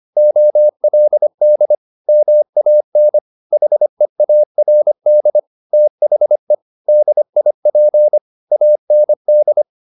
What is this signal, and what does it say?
Morse code, heard as a keyed tone, about 25 wpm, 600 Hz. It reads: OLD MAN HEARD THE DIP AND
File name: the old man and the sea part 1 25wpm